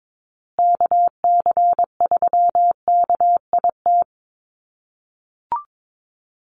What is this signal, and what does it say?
Morse code, heard as a keyed tone, about 22 wpm, 700 Hz. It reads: KC3KIT
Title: USA Call Signs Repeated 3x 22wpm